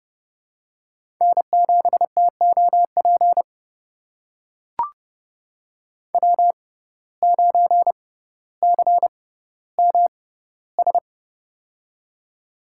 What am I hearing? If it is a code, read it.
Morse code, heard as a keyed tone, about 30 wpm, spaced as for 15 wpm, 700 Hz. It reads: N7TOP W9CMS
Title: USA Call Signs - 5 Characters - 6x Character Spacing - 30wpm